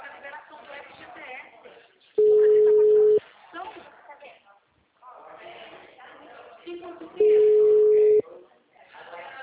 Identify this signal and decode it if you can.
Ringback tone (Europe, 425 Hz), with speech.
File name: out- -2008- - - .99169